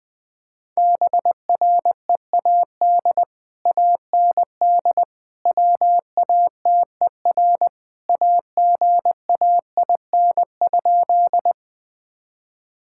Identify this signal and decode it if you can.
Morse code, heard as a keyed tone, about 20 wpm, 700 Hz. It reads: BREAD AND WATER AGAIN?
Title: Sentences from Top 1600 Words 20wpm